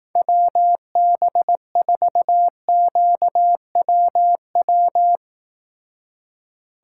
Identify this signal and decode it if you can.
Morse code, heard as a keyed tone, about 18 wpm, 700 Hz. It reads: WB4QWW
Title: Speed Racing - USA 2x3 Call Signs 15wpm